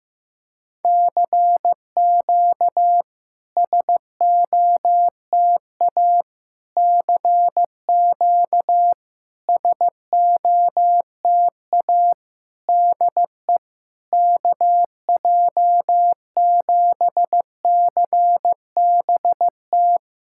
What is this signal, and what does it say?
Morse code, heard as a keyed tone, about 15 wpm, 690 Hz. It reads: CQ SOTA CQ SOTA DE KJ7CBT
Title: SOTA QSO Practice 15wpm